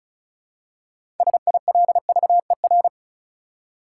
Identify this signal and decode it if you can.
Morse code, heard as a keyed tone, about 36 wpm, 700 Hz. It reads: SILVER
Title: Six Letter Words - 35wpm